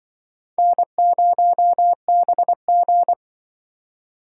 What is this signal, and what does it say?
Morse code, heard as a keyed tone, about 25 wpm, 700 Hz. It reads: N0BG